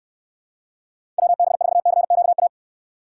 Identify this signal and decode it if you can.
Morse code, heard as a keyed tone, about 69 wpm, 700 Hz. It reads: WB4CQA